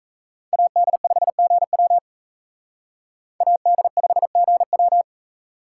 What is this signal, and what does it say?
Morse code, heard as a keyed tone, about 34 wpm, 700 Hz. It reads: AD5GW AD5GW
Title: Speed Racing - USA 2x2 Call Signs 28wpm